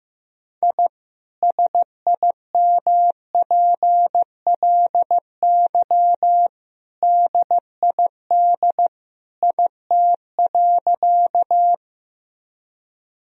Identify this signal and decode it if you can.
Morse code, heard as a keyed tone, about 15 wpm, 700 Hz. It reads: I SIMPLY DID IT.
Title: Sentences from Top 1000 Words - Review Part 4 - 15wpm